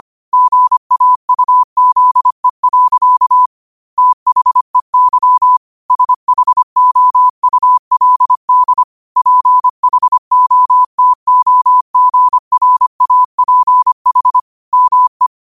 Morse code, heard as a keyed tone, about 25 wpm, 1000 Hz. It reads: GAUZE. THEY SHOULD PHOTOGRAPH ME
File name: qotdfiles / 25wpm